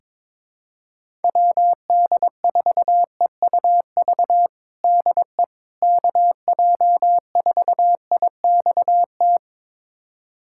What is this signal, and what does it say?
Morse code, heard as a keyed tone, about 22 wpm, 700 Hz. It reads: WD4EUV DE KJ4IXT
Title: USA Callsign DE Callsign - 6 Characters - 22wpm